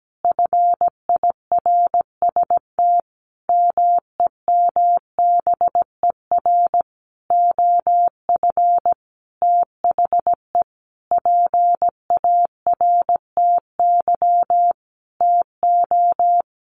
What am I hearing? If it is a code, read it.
Morse code, heard as a keyed tone, about 17 wpm, 700 Hz. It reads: FIRST MEMBER OF THE PARTY TO
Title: Sentences from Top 1400 Words 17wpm